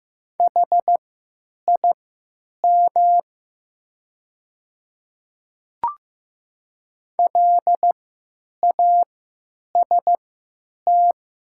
Morse code, heard as a keyed tone, about 15 wpm, spaced as for 9 wpm, 700 Hz. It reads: HIM LAST
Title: 3x Character Spacing - Top 500 Words - 15wpm